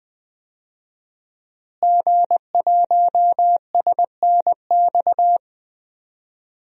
Morse code, heard as a keyed tone, about 20 wpm, 700 Hz. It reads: G1SNX
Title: DXCC-100 Call Signs - 5 Characters - Repeated 3x - 20wpm